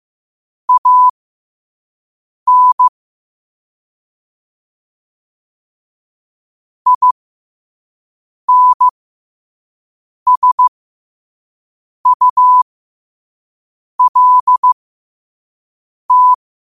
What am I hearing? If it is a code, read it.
Morse code, heard as a keyed tone, about 15 wpm, spaced as for 5 wpm, 1000 Hz. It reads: AN INSULT